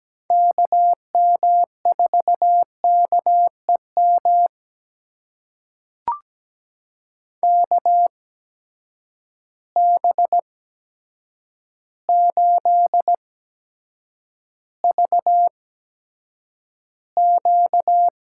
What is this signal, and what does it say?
Morse code, heard as a keyed tone, about 17 wpm, spaced as for 8 wpm, 700 Hz. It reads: KM4KEM KB8VQ
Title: USA Call Signs - 6 Characters - 8x Character Spacing - 17wpm